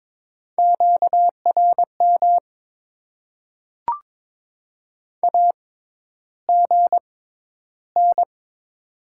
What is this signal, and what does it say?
Morse code, heard as a keyed tone, about 22 wpm, spaced as for 11 wpm, 700 Hz. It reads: QRM AGN
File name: Top 90 QSO Elements - 6x Character Spacing - 22wpm